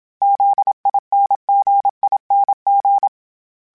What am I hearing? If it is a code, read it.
Morse code, heard as a keyed tone, about 27 wpm, 800 Hz. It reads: ZINGING
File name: Special INGs / Special INGs-mp3s / Dble ings @25